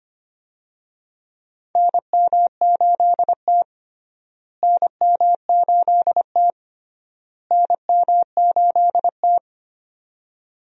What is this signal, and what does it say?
Morse code, heard as a keyed tone, about 25 wpm, 700 Hz. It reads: NM8T NM8T NM8T
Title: USA Call Signs - 4 Characters - Repeated 3x - 25wpm